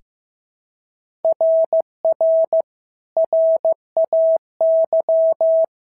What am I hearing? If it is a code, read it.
Morse code, heard as a keyed tone, about 14 wpm, 650 Hz. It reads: RR RAY
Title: SOTA QSO Practice 15wpm